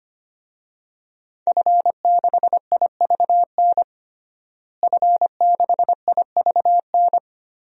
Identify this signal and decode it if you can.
Morse code, heard as a keyed tone, about 25 wpm, 700 Hz. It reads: F6IVN F6IVN
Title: DXCC-100 Call Signs - 5 Characters - Repeated 3x - 25wpm